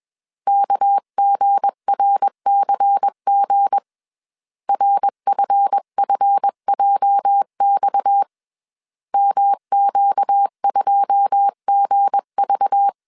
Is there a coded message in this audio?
Yes: Morse code from 0.5 to 12.9 s, heard as a keyed tone, about 21 wpm, 800 Hz. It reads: KGRCG RFFJX MQ2GV